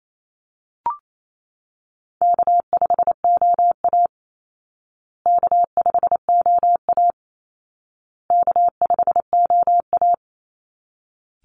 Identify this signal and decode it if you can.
Morse code, heard as a keyed tone, about 28 wpm, 700 Hz. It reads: K5OA K5OA K5OA
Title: Easy Call Signs Repeated 3x - 28wpm